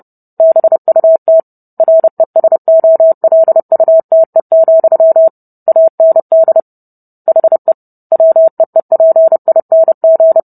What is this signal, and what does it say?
Morse code, heard as a keyed tone, about 30 wpm, 650 Hz. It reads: BUT RESOLUTE, AND HE WEEPING